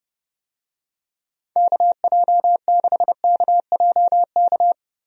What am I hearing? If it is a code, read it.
Morse code, heard as a keyed tone, about 30 wpm, 700 Hz. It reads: KJ6KJK
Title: USA Call Signs - 6 Characters - 30wpm